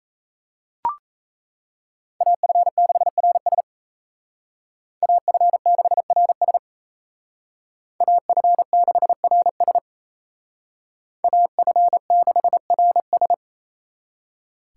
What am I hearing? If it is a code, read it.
Morse code, heard as a keyed tone, about 35 wpm, 700 Hz. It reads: AF6RS AF6RS AF6RS AF6RS
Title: Speed Racing - USA 2x2 Call Signs 28wpm